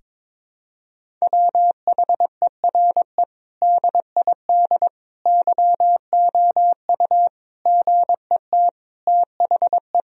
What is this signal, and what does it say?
Morse code, heard as a keyed tone, about 22 wpm, 700 Hz. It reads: WHERE DID YOU GET THE
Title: Sentences from Top 1000 Words - Review Part 4 - 22wpm